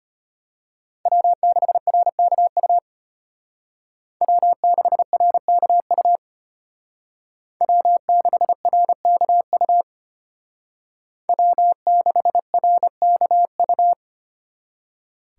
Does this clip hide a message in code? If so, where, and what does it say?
Yes: Morse code from 1.1 to 13.9 s, heard as a keyed tone, about 33 wpm, 700 Hz. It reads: W6RKU W6RKU W6RKU W6RKU